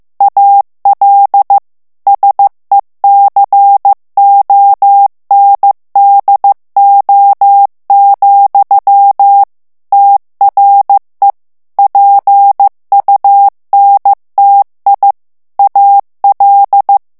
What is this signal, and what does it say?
Morse code, heard as a keyed tone, about 15 wpm, 800 Hz. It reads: AL SECONDO, TRE PUNTI AL
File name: morse mp3 / ITALIEN texte clair 2 vitesse 14 mots minute